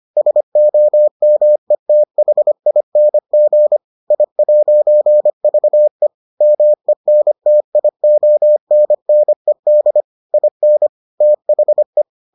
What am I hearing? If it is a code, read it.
Morse code, heard as a keyed tone, about 25 wpm, 600 Hz. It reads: SOMETHING I'VE MENTIONNED IN THE